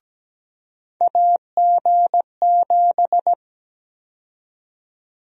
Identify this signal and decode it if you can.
Morse code, heard as a keyed tone, about 17 wpm, 700 Hz. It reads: AG7
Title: USA Call Sign Prefixes - 17wpm